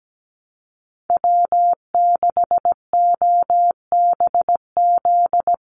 Morse code, heard as a keyed tone, about 17 wpm, 700 Hz. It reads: W6OBZ
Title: USA Call Signs 17wpm